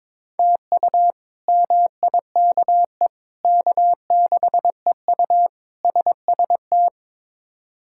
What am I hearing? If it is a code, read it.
Morse code, heard as a keyed tone, about 21 wpm, 700 Hz. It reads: TU MIKE K6EU SST
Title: K1USN SST QSO Practice - 22wpm